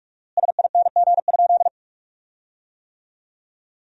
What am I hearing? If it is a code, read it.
Morse code, heard as a keyed tone, about 46 wpm, 700 Hz. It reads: SING?
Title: 4x Word-Spacing - Sentences from Top 400 Words - 45wpm